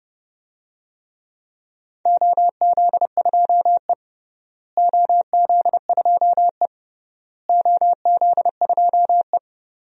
Morse code, heard as a keyed tone, about 30 wpm, 700 Hz. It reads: OZ2E OZ2E OZ2E
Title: DXCC-100 Call Signs - 4 Characters - Repeated 3x - 30wpm